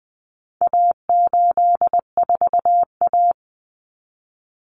Morse code, heard as a keyed tone, about 20 wpm, 700 Hz. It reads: A84A
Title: Common Cut Numbers - 4-Digit 20wpm